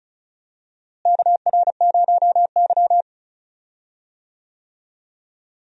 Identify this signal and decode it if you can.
Morse code, heard as a keyed tone, about 36 wpm, 700 Hz. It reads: KR0Y
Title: USA Call Signs - 4 Characters - Repeated 2x - 35wpm